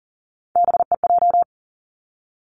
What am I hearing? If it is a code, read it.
Morse code, heard as a keyed tone, about 40 wpm, 700 Hz. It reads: BEJ
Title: Three Challenging Letter-Number Combinations 40wpm